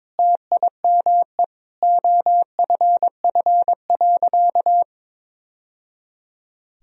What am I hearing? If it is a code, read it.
Morse code, heard as a keyed tone, about 22 wpm, 700 Hz. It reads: TIME OFF.
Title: Sentences from Top 1000 Words - Review Part 3 - 22wpm